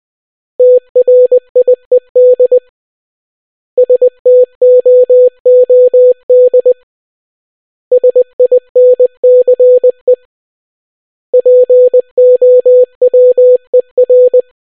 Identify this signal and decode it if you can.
Morse code, heard as a keyed tone, about 20 wpm, 500 Hz. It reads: TRIED STOOD SINCE POWER